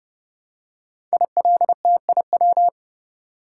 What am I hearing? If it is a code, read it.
Morse code, heard as a keyed tone, about 30 wpm, 700 Hz. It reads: ILTIW
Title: Code Groups - 5 Letters - 30wpm